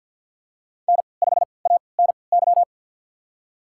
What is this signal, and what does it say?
Morse code, heard as a keyed tone, about 53 wpm, spaced as for 29 wpm, 700 Hz. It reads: N5ANY
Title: USA Call Signs - 5 Characters - 3x Character Spacing - 50wpm